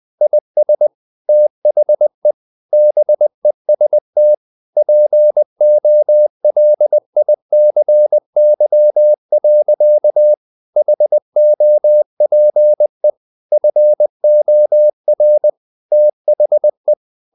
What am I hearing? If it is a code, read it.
Morse code, heard as a keyed tone, about 20 wpm, 600 Hz. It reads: IS THE BEST POLICY. HOPE FOR THE
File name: CW / Practice / EnglishSayings20